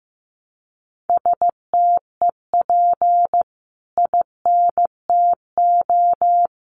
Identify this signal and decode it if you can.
Morse code, heard as a keyed tone, about 15 wpm, 700 Hz. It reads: STEP INTO
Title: Sentences from Top 600 Words 15wpm